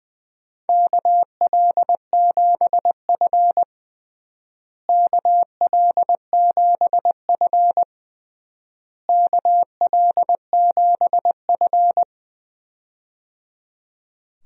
Morse code, heard as a keyed tone, about 20 wpm, 700 Hz. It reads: KL7F KL7F KL7F
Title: DXCC-100 Call Signs - 4 Characters - Repeated 3x - 20wpm